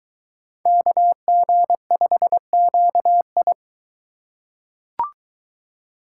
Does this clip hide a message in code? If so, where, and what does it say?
Yes: Morse code from 0.7 to 3.5 s, heard as a keyed tone, about 23 wpm, 700 Hz. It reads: KG5QI